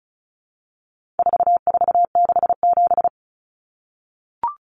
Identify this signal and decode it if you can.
Morse code, heard as a keyed tone, about 34 wpm, 700 Hz. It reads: 4467
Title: Numbers - 4-Digit 35wpm